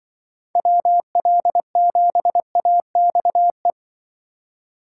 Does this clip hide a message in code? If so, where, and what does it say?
Yes: Morse code from 0.6 to 3.7 s, heard as a keyed tone, about 26 wpm, 700 Hz. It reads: WL7AXE